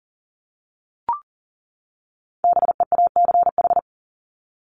Morse code, heard as a keyed tone, about 40 wpm, 700 Hz. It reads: BEACH
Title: Top 1251-1500 Words 40wpm